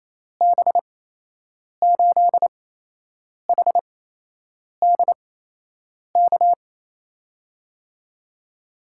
Morse code, heard as a keyed tone, about 28 wpm, spaced as for 8 wpm, 700 Hz. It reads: B8HDK
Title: USA Call Signs - 6 Characters - 8x Character Spacing - 28wpm